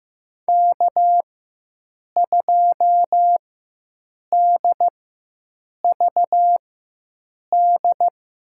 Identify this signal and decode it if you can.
Morse code, heard as a keyed tone, about 15 wpm, spaced as for 7 wpm, 700 Hz. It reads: K2DVD
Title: USA Call Signs - 5 Characters - 4x Character Spacing - 15wpm